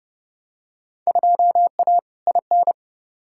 Morse code, heard as a keyed tone, about 30 wpm, 700 Hz. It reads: 2A IN
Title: Field Day Exchanges - 30wpm